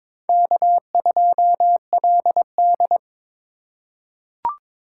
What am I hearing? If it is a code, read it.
Morse code, heard as a keyed tone, about 22 wpm, 700 Hz. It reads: K2LD
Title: USA Call Signs - 4 Characters - 4x Character Spacing - 22wpm